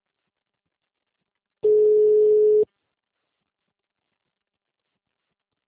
Ringback tone (Europe, 425 Hz).